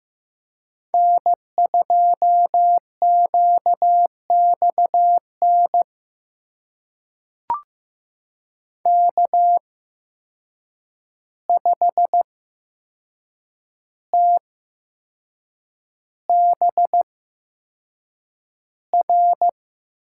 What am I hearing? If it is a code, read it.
Morse code, heard as a keyed tone, about 15 wpm, spaced as for 6 wpm, 700 Hz. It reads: N2QXN K5TBR
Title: USA Call Signs - 5 Characters - 8x Character Spacing - 15wpm